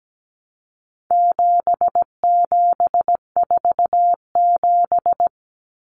Morse code, heard as a keyed tone, about 17 wpm, 700 Hz. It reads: 7747
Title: Common Cut Numbers - 4-Digit 17wpm